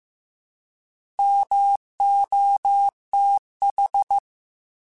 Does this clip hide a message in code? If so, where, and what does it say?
Yes: Morse code from 1.2 to 4.2 s, heard as a keyed tone, about 15 wpm, 800 Hz. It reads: MOTH